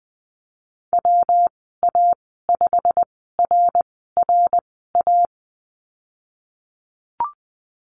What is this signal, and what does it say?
Morse code, heard as a keyed tone, about 20 wpm, spaced as for 15 wpm, 700 Hz. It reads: WA5RRA
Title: 122 Lesson 20g - Number 3 - Call Signs 20wpm